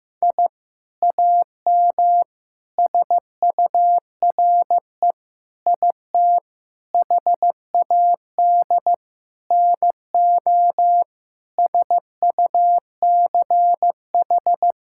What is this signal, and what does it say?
Morse code, heard as a keyed tone, about 15 wpm, 700 Hz. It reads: I AM SURE IT HAD NO SUCH
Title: Sentences from Top 1000 Words - Review Part 6 - 15wpm